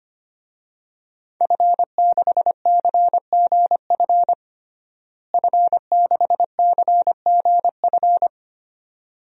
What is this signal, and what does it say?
Morse code, heard as a keyed tone, about 25 wpm, 700 Hz. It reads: F6CGF F6CGF
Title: DXCC-100 Call Signs - 5 Characters - Repeated 2x - 25wpm